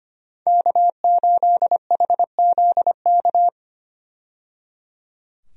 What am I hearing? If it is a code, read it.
Morse code, heard as a keyed tone, about 25 wpm, 690 Hz. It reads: K8HZK
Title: USA Call Signs - 5 Characters - 25wpm